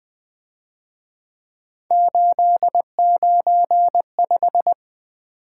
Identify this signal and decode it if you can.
Morse code, heard as a keyed tone, about 24 wpm, 700 Hz. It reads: 895
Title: Speed-Racing - Three Letter-Number Combinations - 20wpm